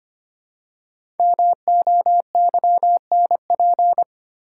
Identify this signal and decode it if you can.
Morse code, heard as a keyed tone, about 25 wpm, 700 Hz. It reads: MOYNP
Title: Code Groups - 5 Letter:Numbers - 25wpm